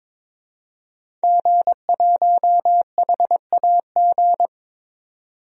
Morse code, heard as a keyed tone, about 22 wpm, 700 Hz. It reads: G1HAG
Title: DXCC-100 Call Signs - 5 Characters - 22wpm